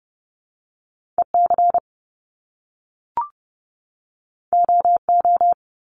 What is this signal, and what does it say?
Morse code, heard as a keyed tone, about 30 wpm, 700 Hz. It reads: EC OO
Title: Top Two Letter Combinations in Words 30wpm